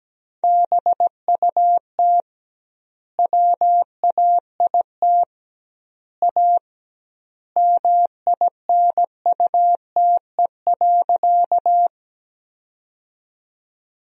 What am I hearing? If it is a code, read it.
Morse code, heard as a keyed tone, about 17 wpm, 700 Hz. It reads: BUT WAIT A MINUTE.
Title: 2x Word-Spacing - Sentences from Top 500 Words - 17wpm